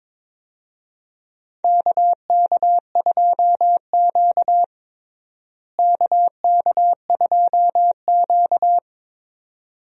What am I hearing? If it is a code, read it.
Morse code, heard as a keyed tone, about 22 wpm, 700 Hz. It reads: KK2Q KK2Q
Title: USA Call Signs - 4 Characters - Repeated 3x - 22wpm